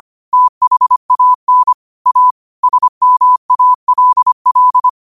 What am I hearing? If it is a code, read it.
Morse code, heard as a keyed tone, about 25 wpm, 1000 Hz. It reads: THAN A SMALL
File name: qotdfiles / 25wpm